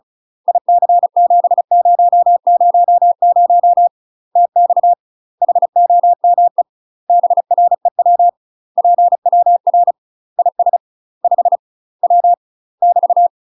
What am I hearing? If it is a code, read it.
Morse code, heard as a keyed tone, about 34 wpm, 700 Hz. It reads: IC7000 TX HOME BREW PWR IS 5 W <BT>
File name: UK MASTER QSOs TRAINING / 35WPM / UK MASTER QSOs 0008